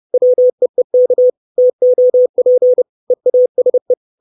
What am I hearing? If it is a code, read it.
Morse code, heard as a keyed tone, about 30 wpm, 500 Hz. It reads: WEEK TOP EASE
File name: 632 Words 30WPM